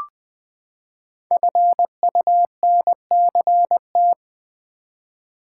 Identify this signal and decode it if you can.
Morse code, heard as a keyed tone, about 20 wpm, 700 Hz. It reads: FUNCT
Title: Build A Word - 5 Letters At A Time - 10-Letter Words - 20wpm